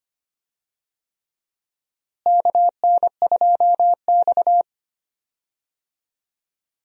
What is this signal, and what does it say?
Morse code, heard as a keyed tone, about 25 wpm, 700 Hz. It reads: KN2X
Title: USA Call Signs - 4 Characters - Repeated 3x - 25wpm